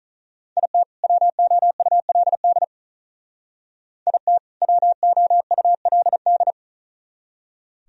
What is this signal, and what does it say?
Morse code, heard as a keyed tone, about 45 wpm, 700 Hz. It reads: IT WOULD IT WOULD